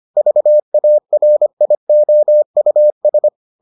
Morse code, heard as a keyed tone, about 25 wpm, 600 Hz. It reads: VARIOUS